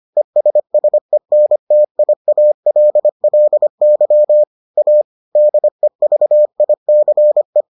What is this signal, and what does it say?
Morse code, heard as a keyed tone, about 25 wpm, 600 Hz. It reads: ESSENTIALLY A DEVICE